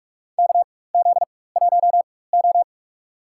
Morse code, heard as a keyed tone, about 43 wpm, spaced as for 21 wpm, 700 Hz. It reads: KZ1O
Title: USA Call Signs - 4 Characters - 4x Character Spacing - 45wpm